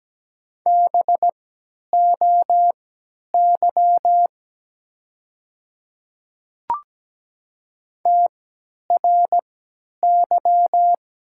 Morse code, heard as a keyed tone, about 17 wpm, spaced as for 10 wpm, 700 Hz. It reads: BOY TRY
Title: 3x Character Spacing - Top 300 Words - 17wpm